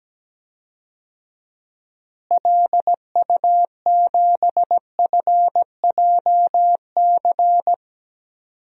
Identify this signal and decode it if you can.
Morse code, heard as a keyed tone, about 17 wpm, 700 Hz. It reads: LU7FJC
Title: DXCC-100 Call Signs - 6 Characters - Repeated 3x - 17wpm